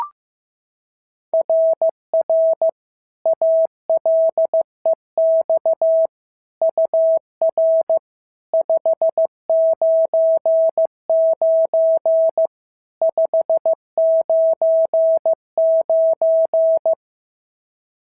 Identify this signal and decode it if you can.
Morse code, heard as a keyed tone, about 15 wpm, 650 Hz. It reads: RR ALEX UR 599 599
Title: SOTA QSO Practice 15wpm